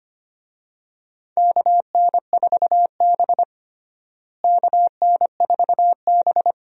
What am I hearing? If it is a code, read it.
Morse code, heard as a keyed tone, about 25 wpm, 700 Hz. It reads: KN4B KN4B
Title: USA Call Signs - 4 Characters - Repeated 3x - 25wpm